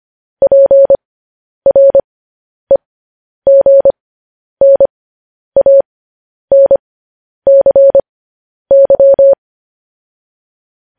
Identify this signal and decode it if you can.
Morse code, heard as a keyed tone, about 25 wpm, spaced as for 10 wpm, 550 Hz. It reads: PREGNANCY
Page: news headlines in morse code at 10 wpm